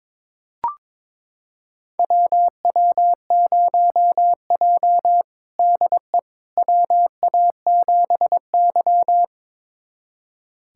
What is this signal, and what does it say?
Morse code, heard as a keyed tone, about 23 wpm, 700 Hz. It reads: WW0J DE WA7Y
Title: USA Callsign DE Callsign - 4 Characters - 22wpm